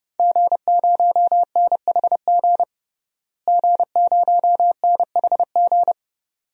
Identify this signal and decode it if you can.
Morse code, heard as a keyed tone, about 30 wpm, 700 Hz. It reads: G0NHG G0NHG
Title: DXCC-100 Call Signs - 5 Characters - Repeated 2x - 30wpm